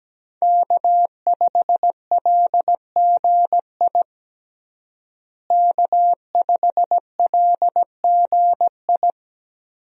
Morse code, heard as a keyed tone, about 17 wpm, 700 Hz. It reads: K5LGI K5LGI
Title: USA Call Signs - 5 Characters - Repeated 3x - 17wpm